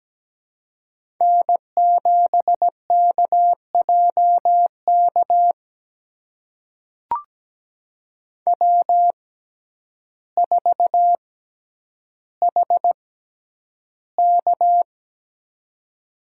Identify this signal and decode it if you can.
Morse code, heard as a keyed tone, about 17 wpm, spaced as for 9 wpm, 700 Hz. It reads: N7KJK W4HK
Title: USA Call Signs - 6x Character Spacing - 17wpm